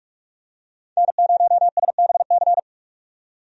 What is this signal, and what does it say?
Morse code, heard as a keyed tone, about 46 wpm, 700 Hz. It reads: N0SBC
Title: USA Call Signs - 5 Characters - 45wpm